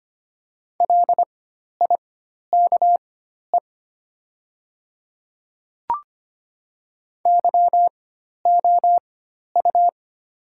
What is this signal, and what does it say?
Morse code, heard as a keyed tone, about 25 wpm, spaced as for 12 wpm, 700 Hz. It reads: LIKE YOU
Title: 4x Character Spacing - Top 100 Words - 25wpm